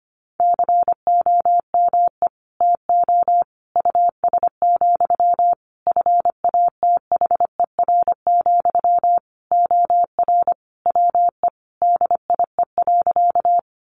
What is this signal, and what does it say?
Morse code, heard as a keyed tone, about 25 wpm, 700 Hz. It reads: COME TO US, FATHER, OR WE DIE.
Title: Sentences from Top 1300 Words 25wpm